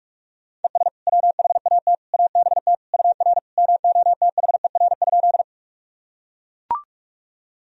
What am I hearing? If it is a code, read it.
Morse code, heard as a keyed tone, about 41 wpm, 700 Hz. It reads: ES WHAT ABT UR MOTHER?
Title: Sentences from Top 200 Words and Abbreviations 45wpm